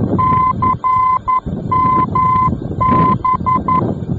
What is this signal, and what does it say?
Morse code, heard as a keyed tone, about 11 wpm, 1020 Hz, with noise. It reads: CMB
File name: france / mx / CMB-F-VOR